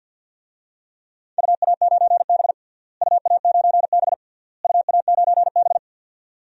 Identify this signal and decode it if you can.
Morse code, heard as a keyed tone, about 48 wpm, 700 Hz. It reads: UA9B UA9B UA9B
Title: DXCC-100 Call Signs - 4 Characters - Repeated 3x - 50wpm